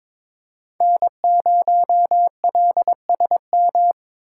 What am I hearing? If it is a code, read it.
Morse code, heard as a keyed tone, about 23 wpm, 700 Hz. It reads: N0LSM
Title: USA Call Signs - 3x Character Spacing - 22wpm